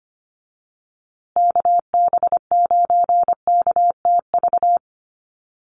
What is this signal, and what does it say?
Morse code, heard as a keyed tone, about 25 wpm, 700 Hz. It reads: KB9KTV
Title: USA Call Signs Repeated 2x 25wpm